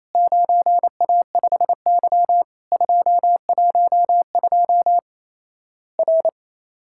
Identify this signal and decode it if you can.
Morse code, heard as a keyed tone, about 28 wpm, 700 Hz. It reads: 9A5Y 212 R
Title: Worked All Europe DX Contest - QTC Exchange - 28wpm